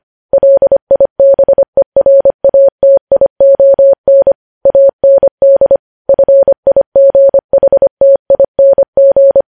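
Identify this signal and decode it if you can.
Morse code, heard as a keyed tone, about 25 wpm, 550 Hz. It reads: LIBERATION AND FIGHTING
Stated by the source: Morse code practice of news headlines at 25 wpm, opening with vvv.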